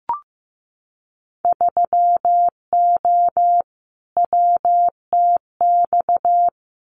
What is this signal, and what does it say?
Morse code, heard as a keyed tone, about 15 wpm, 700 Hz. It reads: 3O WTX
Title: Winter Field Day Exchanges Encore 15wpm